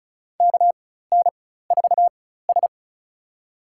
Morse code, heard as a keyed tone, about 34 wpm, spaced as for 16 wpm, 700 Hz. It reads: KN4S
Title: USA Call Signs - 4 Characters - 4x Character Spacing - 35wpm